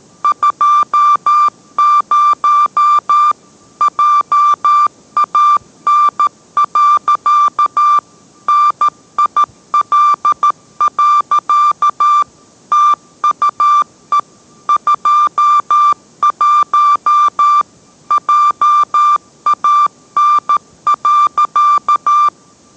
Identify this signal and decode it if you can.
Morse code, heard as a keyed tone, about 13 wpm, 1180 Hz. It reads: 20 JAN. NIL. TUE 21 JAN.